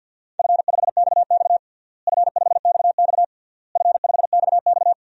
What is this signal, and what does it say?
Morse code, heard as a keyed tone, about 47 wpm, 700 Hz. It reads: F5XX F5XX F5XX
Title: DXCC-100 Call Signs - 4 Characters - Repeated 3x - 50wpm